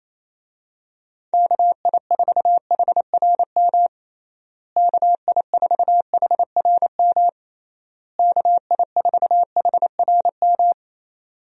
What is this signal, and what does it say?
Morse code, heard as a keyed tone, about 28 wpm, 700 Hz. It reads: KI4HRM KI4HRM KI4HRM
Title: USA Call Signs - 6 Characters - Repeated 3x - 28wpm